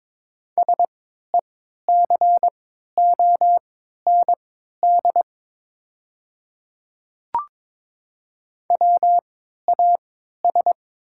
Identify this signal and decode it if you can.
Morse code, heard as a keyed tone, about 22 wpm, spaced as for 12 wpm, 690 Hz. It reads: SECOND WAS
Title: 3x Character Spacing - Top 300 Words - 22wpm